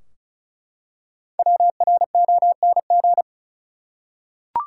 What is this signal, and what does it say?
Morse code, heard as a keyed tone, about 34 wpm, 700 Hz. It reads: WRONG
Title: Five Letter Words - 35wpm